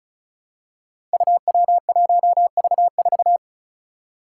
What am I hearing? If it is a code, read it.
Morse code, heard as a keyed tone, about 36 wpm, 700 Hz. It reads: UW1V4
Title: Five Challenging Letter-Number Combinations 35wpm